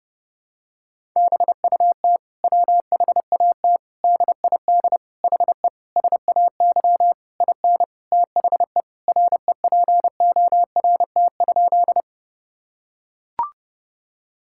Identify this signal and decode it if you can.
Morse code, heard as a keyed tone, about 30 wpm, 700 Hz. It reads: BUT WHAT DID HE SAY IN THE REPORT?
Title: Sentences from Top 1000 Words - Review Part 6 - 30wpm